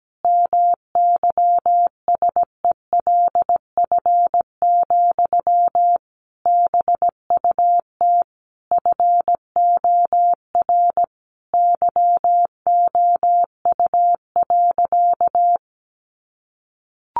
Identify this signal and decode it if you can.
Morse code, heard as a keyed tone, about 17 wpm, 700 Hz. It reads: MYSELF, BUT FOR YOU.
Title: Sentences from Top 1300 Words 17wpm